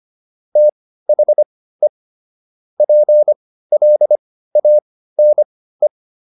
Morse code, heard as a keyed tone, about 25 wpm, spaced as for 15 wpm, 600 Hz. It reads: THE PLANE
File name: SS101 15